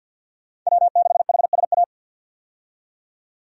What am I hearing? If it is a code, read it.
Morse code, heard as a keyed tone, about 53 wpm, 700 Hz. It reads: W6HSA
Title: USA Call Signs - 5 Characters - 50wpm